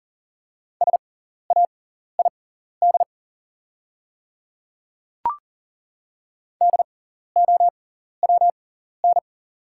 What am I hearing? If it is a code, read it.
Morse code, heard as a keyed tone, about 40 wpm, spaced as for 14 wpm, 690 Hz. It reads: SAID DOWN
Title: 6x Character Spacing - Top 100 Words - 40wpm